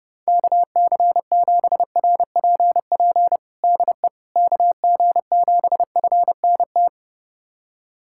Morse code, heard as a keyed tone, about 30 wpm, 700 Hz. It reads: KC7RPP DE KG7FNT
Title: USA Callsign DE Callsign - 6 Characters - 30wpm